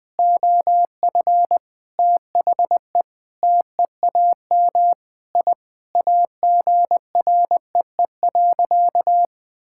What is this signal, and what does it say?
Morse code, heard as a keyed tone, about 21 wpm, 700 Hz. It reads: OF THE TEAM I AGREE.
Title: Sentences with 5-Word Phrases - Spoken Beginning - 20wpm